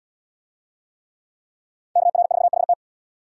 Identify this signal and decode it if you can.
Morse code, heard as a keyed tone, about 77 wpm, 690 Hz. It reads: KR4HT